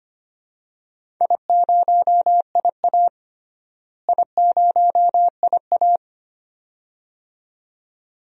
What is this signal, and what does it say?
Morse code, heard as a keyed tone, about 25 wpm, 700 Hz. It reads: I0IA I0IA
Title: DXCC-100 Call Signs - 4 Characters - Repeated 2x - 25wpm